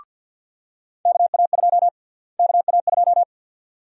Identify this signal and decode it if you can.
Morse code, heard as a keyed tone, about 53 wpm, 700 Hz. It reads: KA2 KA2
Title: USA Call Sign Prefixes - Repeated 2x - 50wpm